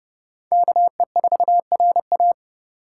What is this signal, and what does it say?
Morse code, heard as a keyed tone, about 30 wpm, 700 Hz. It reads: KE4RA
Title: USA Call Signs - 5 Characters - 3x Character Spacing - 30wpm